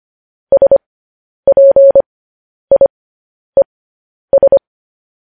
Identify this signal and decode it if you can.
Morse code, heard as a keyed tone, about 25 wpm, spaced as for 10 wpm, 550 Hz. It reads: SPIES
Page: news-headlines-in-morse-code-at-10-wpm